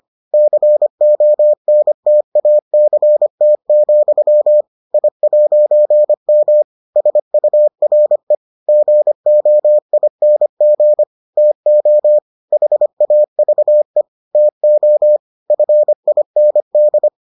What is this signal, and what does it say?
Morse code, heard as a keyed tone, about 25 wpm, 600 Hz. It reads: CONTACT, I'M SURE GOING TO HAVE TO FIND